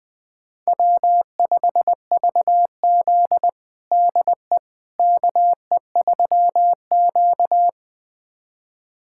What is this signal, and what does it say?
Morse code, heard as a keyed tone, about 20 wpm, 700 Hz. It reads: W5VZ DE KE3Q
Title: USA Callsign DE Callsign - 4 Characters - 20wpm